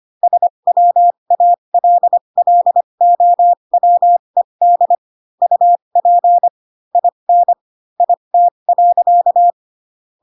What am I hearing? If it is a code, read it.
Morse code, heard as a keyed tone, about 25 wpm, spaced as for 22 wpm, 700 Hz. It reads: SWALLOWED UP IN IT.
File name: Pinocchio / Pinocchio ch3 18 25 ik1hge